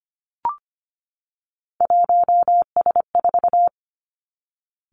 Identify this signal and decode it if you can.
Morse code, heard as a keyed tone, about 25 wpm, 700 Hz. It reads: 1S4
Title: Three Challenging Letter-Number Combinations 25wpm